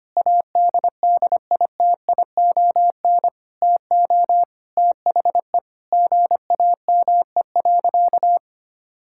Morse code, heard as a keyed tone, about 25 wpm, 700 Hz. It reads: ADDITION TO THE GAME.
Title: Sentences from Top 1000 Words - Review Part 6 - 25wpm